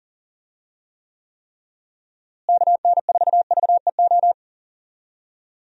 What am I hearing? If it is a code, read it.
Morse code, heard as a keyed tone, about 40 wpm, 700 Hz. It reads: KN4VEO
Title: USA Call Signs - 6 Characters - Repeated 3x - 40wpm